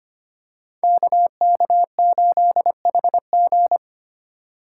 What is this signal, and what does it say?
Morse code, heard as a keyed tone, about 25 wpm, 690 Hz. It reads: KK8HG
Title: Code Groups - 5 Letter:Numbers - 25wpm